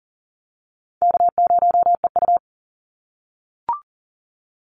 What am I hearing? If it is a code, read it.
Morse code, heard as a keyed tone, about 38 wpm, 700 Hz. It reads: K0EU
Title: Easy Call Signs - 40wpm